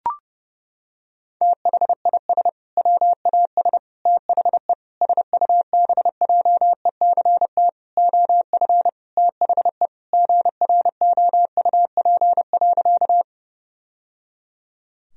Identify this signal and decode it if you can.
Morse code, heard as a keyed tone, about 32 wpm, 700 Hz. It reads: THIS WAS THE SUBJECT OF THE GROUP.
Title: Sentences from Top 1000 Words - Review Part 1 - 30wpm